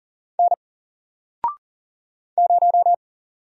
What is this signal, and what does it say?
Morse code, heard as a keyed tone, about 40 wpm, 700 Hz. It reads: N 0
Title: Single Letter-Number Alt 40wpm